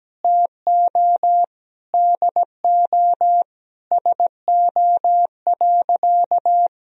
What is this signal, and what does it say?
Morse code, heard as a keyed tone, about 17 wpm, 700 Hz. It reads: TO DO SO.
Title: Sentences from Top 500 Words and Abbreviations 17wpm